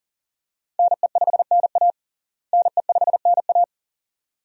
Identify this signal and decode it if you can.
Morse code, heard as a keyed tone, about 38 wpm, 700 Hz. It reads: NE5NA NE5NA
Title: USA Call Signs - 5 Characters - Repeated 2x - 40wpm